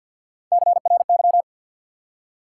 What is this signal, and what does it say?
Morse code, heard as a keyed tone, about 51 wpm, 700 Hz. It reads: CRY
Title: Three Letter Words - 50wpm